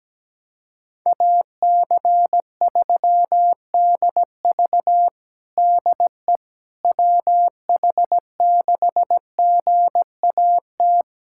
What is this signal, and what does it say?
Morse code, heard as a keyed tone, about 17 wpm, 700 Hz. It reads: AC3DV DE WH6GAT
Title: USA Callsign DE Callsign 17wpm